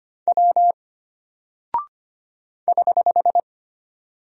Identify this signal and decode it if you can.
Morse code, heard as a keyed tone, about 25 wpm, 700 Hz. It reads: W <HH>
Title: Single Character (Letter:Number:Punctuation:Prosign) Alt 25wpm